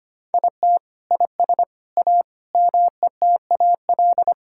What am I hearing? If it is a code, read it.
Morse code, heard as a keyed tone, about 25 wpm, 700 Hz. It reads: IT IS A METAL
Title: Sentences from Top 1000 Words - Review Part 6 - 25wpm